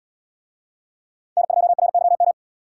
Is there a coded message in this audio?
Yes: Morse code from 1.4 to 2.3 s, heard as a keyed tone, about 70 wpm, 700 Hz. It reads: N2RYM